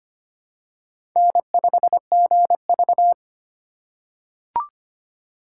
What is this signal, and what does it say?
Morse code, heard as a keyed tone, about 25 wpm, 700 Hz. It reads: N5GV
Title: USA Call Signs - 4x Character Spacing - 25wpm